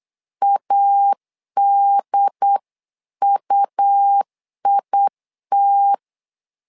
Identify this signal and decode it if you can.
Morse code, heard as a keyed tone, about 8 wpm, 800 Hz. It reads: ADUIT
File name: LEZIONI / 3 MWRO / 3 MW 8wpm 2501